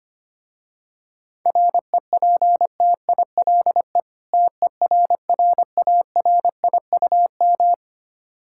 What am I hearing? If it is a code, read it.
Morse code, heard as a keyed tone, about 25 wpm, 700 Hz. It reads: REPTILE TERRARIUM
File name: practice / 25wpm / Difficult Words with Hints - 25wpm